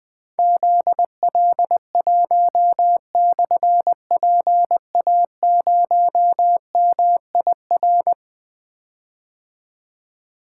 Morse code, heard as a keyed tone, about 20 wpm, 700 Hz. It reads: ZL1/PA0MIR
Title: Difficult Call Signs 20wpm